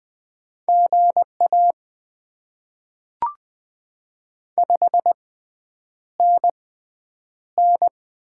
Morse code, heard as a keyed tone, about 20 wpm, spaced as for 9 wpm, 700 Hz. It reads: GA 5NN
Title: Top 20 QSO Elements - 6x Character Spacing - 20wpm